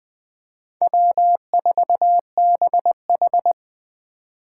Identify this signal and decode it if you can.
Morse code, heard as a keyed tone, about 21 wpm, 700 Hz. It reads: W4BH
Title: Speed Racing - USA 1x2 Call Signs 17wpm